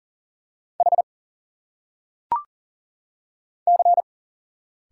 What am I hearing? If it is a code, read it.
Morse code, heard as a keyed tone, about 40 wpm, 700 Hz. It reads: H C
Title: Single Letters - Phonetic Pronunciation 40wpm